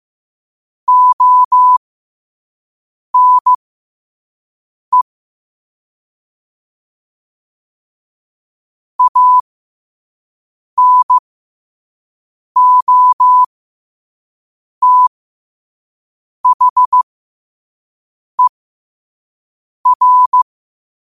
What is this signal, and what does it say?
Morse code, heard as a keyed tone, about 15 wpm, spaced as for 5 wpm, 1000 Hz. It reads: ONE ANOTHER
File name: 5wpm